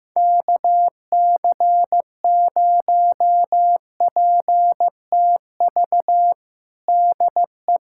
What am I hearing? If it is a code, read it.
Morse code, heard as a keyed tone, about 15 wpm, 700 Hz. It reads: KC0PTV DE
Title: USA Callsign DE Callsign 15wpm